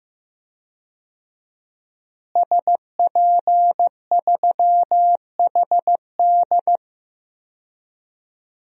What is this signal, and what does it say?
Morse code, heard as a keyed tone, about 15 wpm, 700 Hz. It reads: SP3HD
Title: DXCC-100 Call Signs - 5 Characters - Repeated 3x - 15wpm